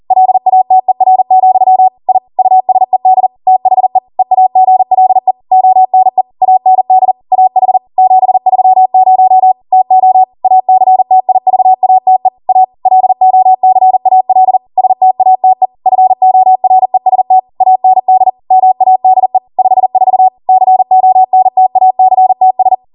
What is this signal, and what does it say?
Morse code, heard as a keyed tone, about 40 wpm, 750 Hz. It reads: LATER, I USED THE EAGLE ONE AND AH 730 TO ACTIVATE A LOCAL STATE FOREST AND MADE 54 CONTACTS